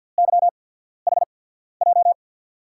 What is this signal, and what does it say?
Morse code, heard as a keyed tone, about 47 wpm, spaced as for 13 wpm, 700 Hz. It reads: YHJ
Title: USA Call Signs - 6 Characters - 8x Character Spacing - 50wpm